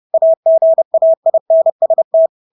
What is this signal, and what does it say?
Morse code, heard as a keyed tone, about 30 wpm, 650 Hz. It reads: AGAINST
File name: Engelske ord 30-30 wpm